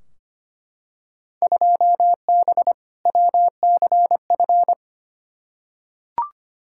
Morse code, heard as a keyed tone, about 25 wpm, 700 Hz. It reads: 2B WCF
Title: Field Day Exchanges Encore - 25wpm